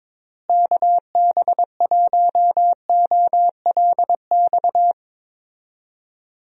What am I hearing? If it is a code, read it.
Morse code, heard as a keyed tone, about 22 wpm, 700 Hz. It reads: KB1OLX
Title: USA Call Signs - 6 Characters - Repeated 2x - 22wpm